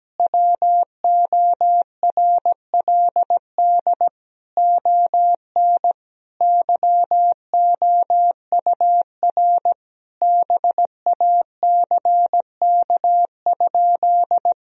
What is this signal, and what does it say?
Morse code, heard as a keyed tone, about 17 wpm, 700 Hz. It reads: WORLD ON YOUR BACK?
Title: Sentences from Top 1000 Words - Review Part 2 - 17wpm